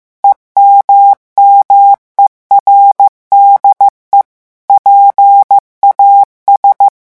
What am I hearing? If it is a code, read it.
Morse code, heard as a keyed tone, about 15 wpm, 800 Hz. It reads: EMMERDE PAS
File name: morse mp3 / revision9 citations humoristiques vitesse 14 mots minute